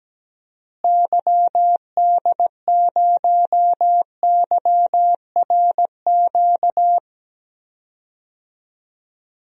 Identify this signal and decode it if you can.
Morse code, heard as a keyed tone, about 17 wpm, 700 Hz. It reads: YD0YRQ
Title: DXCC-100 Call Signs - 6 Characters - Repeated 3x - 17wpm